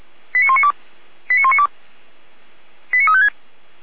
Five-tone selective call (ZVEI), heard as quick runs of five tones, tones of about 70 ms. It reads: 89192 89192 89356